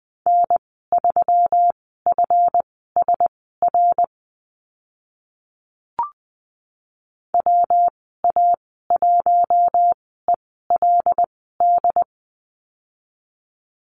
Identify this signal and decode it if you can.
Morse code, heard as a keyed tone, about 20 wpm, spaced as for 14 wpm, 700 Hz. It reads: N3FSR WA1ELD
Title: 140 Lesson 23g - Letter f - Call Signs 20wpm